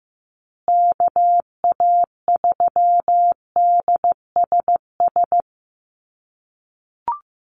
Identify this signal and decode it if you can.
Morse code, heard as a keyed tone, about 15 wpm, 700 Hz. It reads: KA3DSS
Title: USA Call Signs Repeated 3x 15wpm